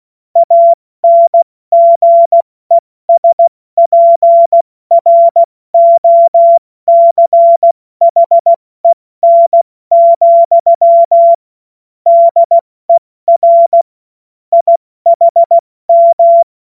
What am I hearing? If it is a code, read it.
Morse code, heard as a keyed tone, about 16 wpm, 680 Hz. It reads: ANGESPROCHEN, DER IHM
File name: Morsekurs / Witze / Witz 25. Oktober 2022 70 F80